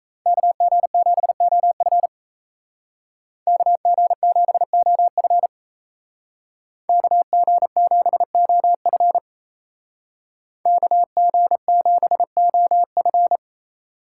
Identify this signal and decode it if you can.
Morse code, heard as a keyed tone, about 35 wpm, 700 Hz. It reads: KG7OF KG7OF KG7OF KG7OF
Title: Speed Racing - USA 2x2 Call Signs 28wpm